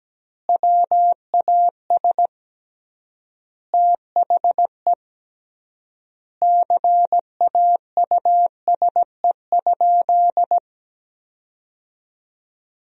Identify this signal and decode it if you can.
Morse code, heard as a keyed tone, about 17 wpm, 700 Hz. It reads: WAS THE CAUSE?
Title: 3x Word-Spacing - Sentences from Top 200 Words - 17wpm